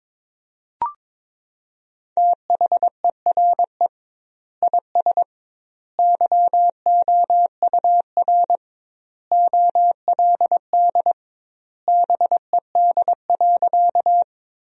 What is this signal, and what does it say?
Morse code, heard as a keyed tone, about 22 wpm, 700 Hz. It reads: THERE IS YOUR OLD BED.
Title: 2x Word-Spacing - Sentences from Top 500 Words - 22wpm